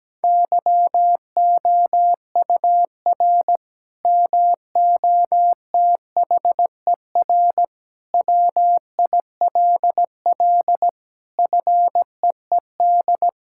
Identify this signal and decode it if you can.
Morse code, heard as a keyed tone, about 17 wpm, 700 Hz. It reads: YOUR MOTHER WILL FEED
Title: Sentences from Top 1600 Words 17wpm